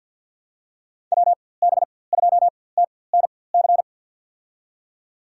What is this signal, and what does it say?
Morse code, heard as a keyed tone, about 51 wpm, spaced as for 23 wpm, 700 Hz. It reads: WB2TNC
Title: USA Call Signs - 4x Character Spacing - 50wpm